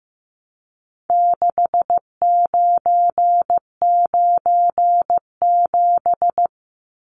Morse code, heard as a keyed tone, about 15 wpm, 700 Hz. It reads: 6997
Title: Numbers - 4-Digit 15wpm